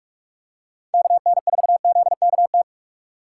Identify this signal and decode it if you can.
Morse code, heard as a keyed tone, about 46 wpm, 700 Hz. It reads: KN4ZKT
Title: K1USN SST QSO Practice - 45wpm